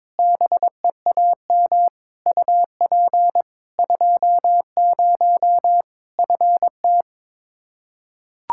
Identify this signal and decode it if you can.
Morse code, heard as a keyed tone, about 22 wpm, 700 Hz. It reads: BEAM UP 20 FT
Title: Practice QSO - Exchange 2 - Info - 22wpm